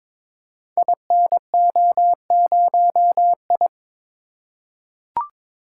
Morse code, heard as a keyed tone, about 22 wpm, 700 Hz. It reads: INO0I
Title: Code Groups - 5 Letter:Numbers - 22wpm